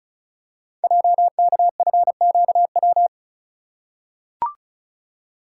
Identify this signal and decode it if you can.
Morse code, heard as a keyed tone, about 36 wpm, 700 Hz. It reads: JKFQW
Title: Code Groups - 5 Letters - 35wpm